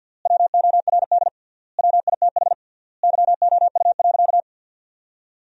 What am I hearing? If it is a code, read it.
Morse code, heard as a keyed tone, about 47 wpm, 700 Hz. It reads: WORD WITH YOU.
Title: 3x Word-Spacing - Sentences from Top 300 Words - 50wpm